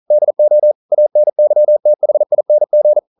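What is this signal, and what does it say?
Morse code, heard as a keyed tone, about 40 wpm, 600 Hz. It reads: DO ANYTHING
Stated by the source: Morse code practice at 41 wpm, from CW med Gnister.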